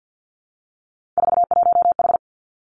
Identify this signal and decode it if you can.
Morse code, heard as a keyed tone, about 50 wpm, 700 Hz. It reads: 41H
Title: Three Challenging Letter-Number Combinations 50wpm